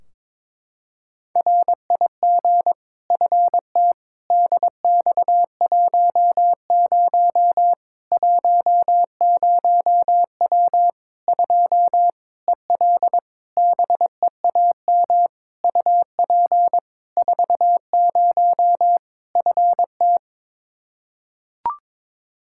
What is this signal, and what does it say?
Morse code, heard as a keyed tone, about 21 wpm, 700 Hz. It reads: RIG FT DX10 10W 2 EL BEAM UP 40 FT
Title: Practice QSO - Exchange 2 - Info - 22wpm